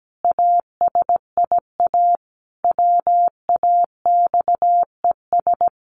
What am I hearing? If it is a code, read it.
Morse code, heard as a keyed tone, about 17 wpm, 700 Hz. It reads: ASIA WAXES
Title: A Sound Of Thunder 17wpm